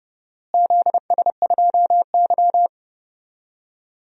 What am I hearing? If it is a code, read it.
Morse code, heard as a keyed tone, about 30 wpm, 700 Hz. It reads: ZS2Y
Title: DXCC-100 Call Signs - 4 Characters - Repeated 3x - 30wpm